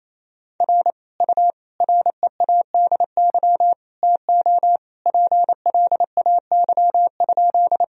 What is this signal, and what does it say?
Morse code, heard as a keyed tone, about 27 wpm, 700 Hz. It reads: R U READY TO PLAY?
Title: Sentences from Top 400 Words and Abbreviations 28wpm